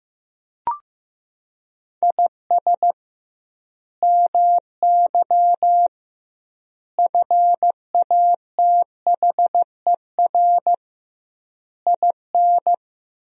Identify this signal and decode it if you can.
Morse code, heard as a keyed tone, about 15 wpm, 700 Hz. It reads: IS MY FATHER IN
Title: 2x Word-Spacing - Sentences from Top 500 Words - 15wpm